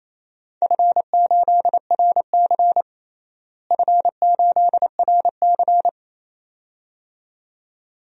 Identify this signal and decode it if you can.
Morse code, heard as a keyed tone, about 28 wpm, 700 Hz. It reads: F8RC F8RC
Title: DXCC-100 Call Signs - 4 Characters - Repeated 2x - 28wpm